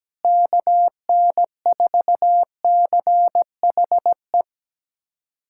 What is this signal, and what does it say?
Morse code, heard as a keyed tone, about 17 wpm, 700 Hz. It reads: KN4CHE
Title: USA Callsign DE Callsign - 6 Characters - 17wpm